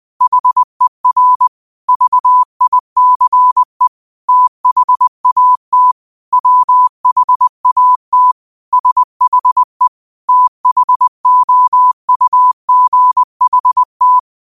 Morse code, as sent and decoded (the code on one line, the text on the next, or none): .... .-. / ...- .. -.-. / - .... .- - / .-- .... .- - / ... .... / - .... --- ..- --. .... -
HER VICE THAT WHAT SHE THOUGHT